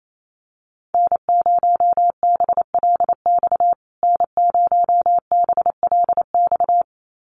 Morse code, heard as a keyed tone, about 28 wpm, 700 Hz. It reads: N0BLX N0BLX
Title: USA Call Signs Repeated 2x 28wpm